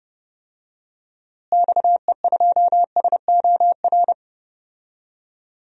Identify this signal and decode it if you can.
Morse code, heard as a keyed tone, about 31 wpm, 700 Hz. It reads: XE2SOR
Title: DXCC-100 Call Signs - 6 Characters - Repeated 2x - 30wpm